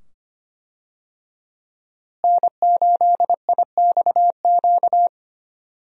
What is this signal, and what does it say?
Morse code, heard as a keyed tone, about 25 wpm, 700 Hz. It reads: N8IXQ
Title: USA Call Signs - 5 Characters - Repeated 2x - 25wpm